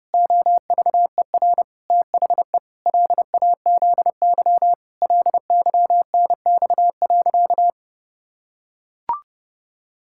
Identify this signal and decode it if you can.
Morse code, heard as a keyed tone, about 32 wpm, 700 Hz. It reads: OVER THE LAZY LYNX.
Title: Pangram Challenge v2 - 30wpm